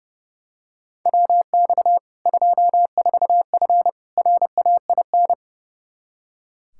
Morse code, heard as a keyed tone, about 30 wpm, 700 Hz. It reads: WX 24F RAIN
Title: K1USN SST QSO Practice 30wpm